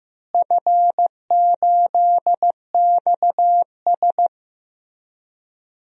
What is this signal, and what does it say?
Morse code, heard as a keyed tone, about 15 wpm, 700 Hz. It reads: F8XS
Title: DXCC-100 Call Signs - 4 Characters - Repeated 3x - 15wpm